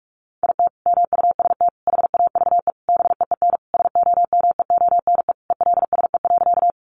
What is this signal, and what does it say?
Morse code, heard as a keyed tone, about 44 wpm, 700 Hz. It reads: IT MUST HAVE BEEN SOMEONE ELSE.
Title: Sentences from Top 800 Words 45wpm